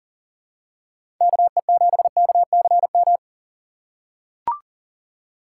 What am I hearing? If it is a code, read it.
Morse code, heard as a keyed tone, about 40 wpm, 700 Hz. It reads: KE7KCM
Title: USA Call Signs - 6 Characters - 4x Character Spacing - 40wpm